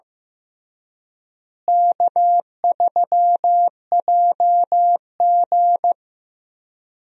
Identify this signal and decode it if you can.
Morse code, heard as a keyed tone, about 16 wpm, 700 Hz. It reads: K3JG
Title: USA Call Signs - 4 Characters - Repeated 3x - 15wpm